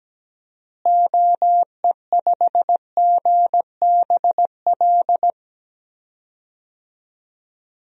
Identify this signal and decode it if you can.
Morse code, heard as a keyed tone, about 17 wpm, 700 Hz. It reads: OE5GBL
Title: DXCC-100 Call Signs - 6 Characters - Repeated 2x - 17wpm